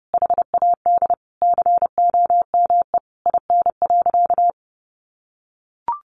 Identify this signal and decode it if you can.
Morse code, heard as a keyed tone, about 30 wpm, 700 Hz. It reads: HAD COME IN.
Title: Sentences from Top 1000 Words 30wpm